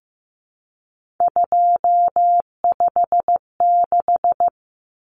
Morse code, heard as a keyed tone, about 15 wpm, 700 Hz. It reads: 256
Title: Common Cut Numbers - 3-Digit 15wpm